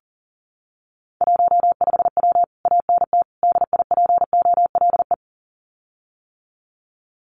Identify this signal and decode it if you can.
Morse code, heard as a keyed tone, about 40 wpm, 700 Hz. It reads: 15W ANT DIPOLE
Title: Common QSO Phrases Encore 40wpm